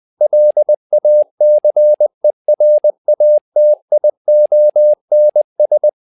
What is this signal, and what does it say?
Morse code, heard as a keyed tone, about 20 wpm, 600 Hz. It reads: LACERATIONS
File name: tower trouble 20wpm